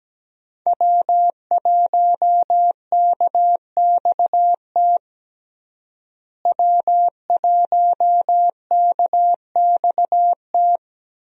Morse code, heard as a keyed tone, about 17 wpm, 700 Hz. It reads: W1KXT W1KXT
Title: USA Call Signs - 5 Characters - Repeated 2x - 17wpm